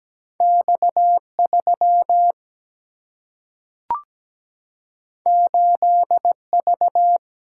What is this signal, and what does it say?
Morse code, heard as a keyed tone, about 17 wpm, 700 Hz. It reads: X3 8V
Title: Single-Letter Single-Number 17wpm